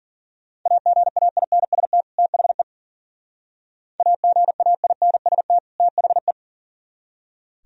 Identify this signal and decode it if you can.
Morse code, heard as a keyed tone, about 48 wpm, 690 Hz. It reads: AGAINST THE AGAINST THE
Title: Speed-Racing - Sets of 2 Words 40wpm